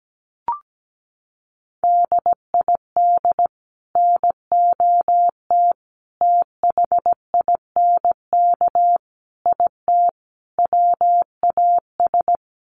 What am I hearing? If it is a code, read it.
Morse code, heard as a keyed tone, about 17 wpm, 700 Hz. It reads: DID NOT THINK IT WAS
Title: Sets of 5 Words Encore 17wpm